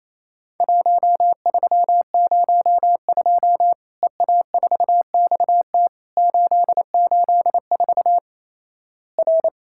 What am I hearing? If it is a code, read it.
Morse code, heard as a keyed tone, about 28 wpm, 700 Hz. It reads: 1302 EA4XT 884 R
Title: Worked All Europe DX Contest - QTC Exchange - 28wpm